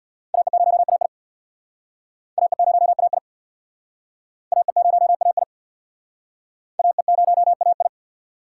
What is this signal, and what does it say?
Morse code, heard as a keyed tone, about 68 wpm, 700 Hz. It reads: AE0AI AE0AI AE0AI AE0AI